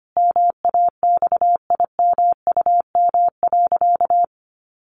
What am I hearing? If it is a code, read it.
Morse code, heard as a keyed tone, about 25 wpm, 700 Hz. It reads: MAXIMUM.
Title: Sentences from Top 1300 Words 25wpm